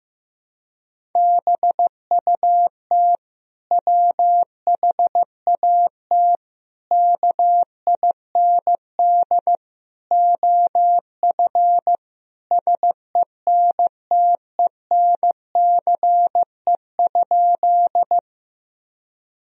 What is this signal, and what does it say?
Morse code, heard as a keyed tone, about 14 wpm, 700 Hz. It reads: BUT WHAT KIND OF SENTENCE?
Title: Sentences from Top 1000 Words - Review Part 1 - 15wpm